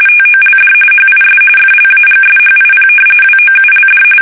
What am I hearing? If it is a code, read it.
RTTY, about 45.45 baud. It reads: TOLD HIS FLAGSHIP WOULD
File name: fsk850 / battleshipnj